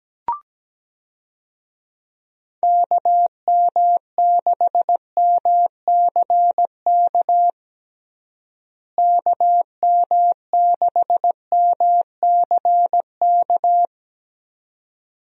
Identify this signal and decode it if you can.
Morse code, heard as a keyed tone, about 17 wpm, 700 Hz. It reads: KM6MCK KM6MCK
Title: USA Call Signs - 6 Characters - Repeated 2x - 17wpm